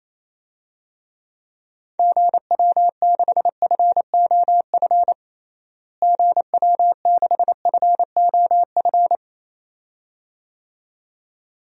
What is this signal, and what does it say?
Morse code, heard as a keyed tone, about 27 wpm, 700 Hz. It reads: GW6FOF GW6FOF
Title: DXCC-100 Call Signs - 6 Characters - Repeated 2x - 28wpm